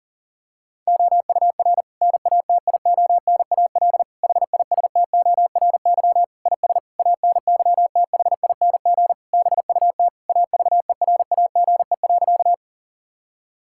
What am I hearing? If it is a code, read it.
Morse code, heard as a keyed tone, about 40 wpm, 700 Hz. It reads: OUR NATIONAL HISTORY IS ANYTHING BUT AVERAGE.
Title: Sentences from Top 1000 Words - Review Part 6 - 40wpm